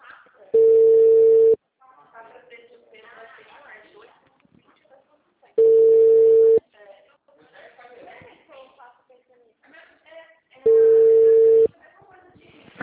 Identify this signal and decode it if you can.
Ringback tone (China, 450 Hz), with speech.
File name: out- -2013- - - .74770